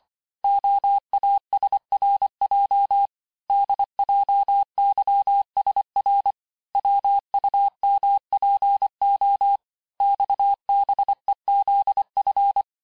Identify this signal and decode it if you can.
Morse code, heard as a keyed tone, about 25 wpm, 800 Hz. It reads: OASRJ DJYSR WUMPO XBEZF
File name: texte-chiffre-vitesse-23mpm